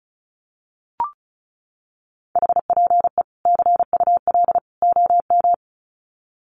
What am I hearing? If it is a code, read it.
Morse code, heard as a keyed tone, about 34 wpm, 700 Hz. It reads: HPE CUL OM
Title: Common QSO Phrases 35wpm